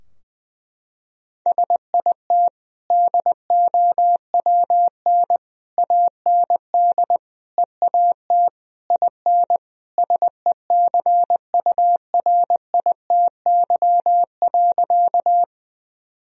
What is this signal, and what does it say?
Morse code, heard as a keyed tone, about 20 wpm, 700 Hz. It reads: SIT DOWN AND EAT IN SECURITY.
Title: Sentences from Top 1000 Words - Review Part 4 - 20wpm